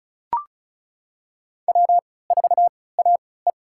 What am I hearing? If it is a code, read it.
Morse code, heard as a keyed tone, about 34 wpm, spaced as for 20 wpm, 690 Hz. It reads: W4AE
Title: USA Call Signs - 3x Character Spacing - 35wpm